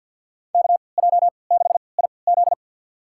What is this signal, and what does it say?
Morse code, heard as a keyed tone, about 51 wpm, spaced as for 28 wpm, 700 Hz. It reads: KJ6IZ
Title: USA Call Signs - 3x Character Spacing - 50wpm